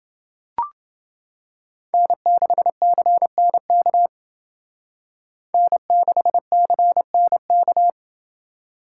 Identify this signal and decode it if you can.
Morse code, heard as a keyed tone, about 29 wpm, 700 Hz. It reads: N6CNK N6CNK